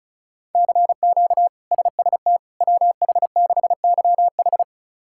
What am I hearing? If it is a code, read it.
Morse code, heard as a keyed tone, about 34 wpm, 700 Hz. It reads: CQ SST WH6YH
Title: K1USN SST QSO Practice - 35wpm